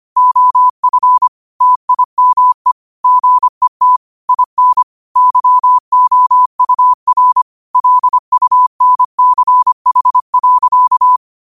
Morse code, heard as a keyed tone, about 25 wpm, 1000 Hz. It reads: OF TIME GET IN YOUR LUNCH.